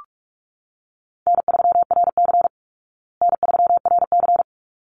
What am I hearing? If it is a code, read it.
Morse code, heard as a keyed tone, about 44 wpm, 700 Hz. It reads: N3RC N3RC
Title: Easy Call Signs Repeated 2x - 45wpm